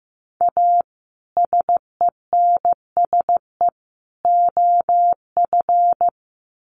Morse code, heard as a keyed tone, about 15 wpm, 700 Hz. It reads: A SENSE OF
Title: Sets of 3 Words 15wpm